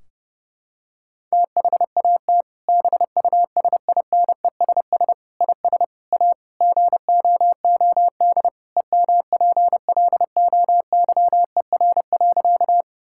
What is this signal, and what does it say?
Morse code, heard as a keyed tone, about 30 wpm, 700 Hz. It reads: THAT BUSINESS IS A GOOD EMPLOYER.
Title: Sentences from Top 1600 Words 30wpm